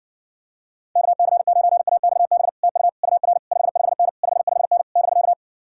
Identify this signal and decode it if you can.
Morse code, heard as a keyed tone, about 60 wpm, 700 Hz. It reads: KC9AXB TU UR 55N 55N <BK>